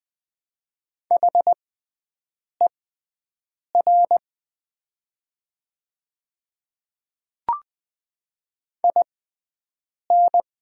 Morse code, heard as a keyed tone, about 20 wpm, spaced as for 7 wpm, 700 Hz. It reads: HER IN
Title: 6x Character Spacing - Top 100 Words - 20wpm